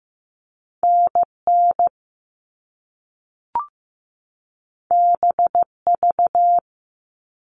Morse code, heard as a keyed tone, about 15 wpm, 700 Hz. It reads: NN BV
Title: All Cut Numbers - 2-Digit 15wpm